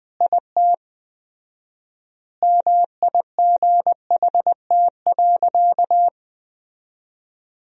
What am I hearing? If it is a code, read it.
Morse code, heard as a keyed tone, about 21 wpm, 700 Hz. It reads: IT MIGHT.
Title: 4x Word-Spacing - Sentences from Top 500 Words - 20wpm